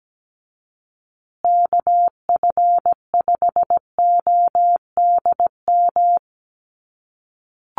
Morse code, heard as a keyed tone, about 17 wpm, 700 Hz. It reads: KF5ODM
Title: USA Call Signs Repeated 3x 17wpm